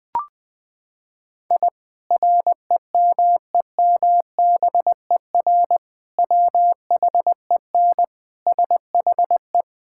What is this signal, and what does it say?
Morse code, heard as a keyed tone, about 21 wpm, 700 Hz. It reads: I REMEMBER WHEN SHE
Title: Sentences with 2-Word Phrases - 20wpm